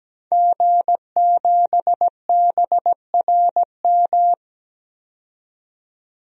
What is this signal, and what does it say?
Morse code, heard as a keyed tone, about 17 wpm, 700 Hz. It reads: G7BRM
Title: DXCC-100 Call Signs - 5 Characters - 17wpm